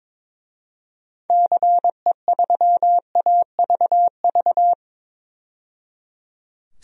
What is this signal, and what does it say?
Morse code, heard as a keyed tone, about 21 wpm, 700 Hz. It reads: CE3AVV
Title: DXCC-100 Call Signs - 6 Characters - 22wpm